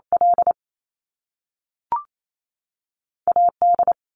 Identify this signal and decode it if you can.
Morse code, heard as a keyed tone, about 28 wpm, 700 Hz. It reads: L AD
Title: Top Two Letter Combinations in Words 28wpm